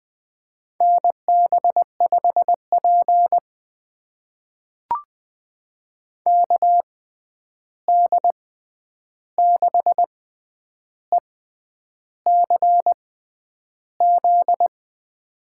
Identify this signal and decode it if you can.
Morse code, heard as a keyed tone, about 20 wpm, spaced as for 9 wpm, 700 Hz. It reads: NB5P KD6ECZ
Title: USA Call Signs - 6x Character Spacing - 20wpm